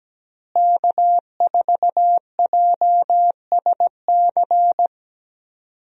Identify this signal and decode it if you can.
Morse code, heard as a keyed tone, about 17 wpm, 700 Hz. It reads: K4JSC
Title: USA Call Signs - 5 Characters - Repeated 3x - 17wpm